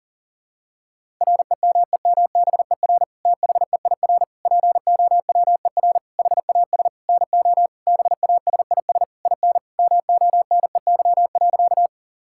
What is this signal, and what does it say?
Morse code, heard as a keyed tone, about 40 wpm, 700 Hz. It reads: REMEMBER THEIR POWER HAS NO BASIS IN MONEY.
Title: Sentences from Top 1000 Words - Review Part 5 - 40wpm